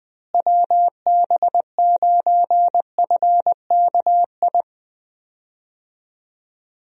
Morse code, heard as a keyed tone, about 20 wpm, 700 Hz. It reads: WB9FKI
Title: USA Call Signs - 6 Characters - Repeated 3x - 20wpm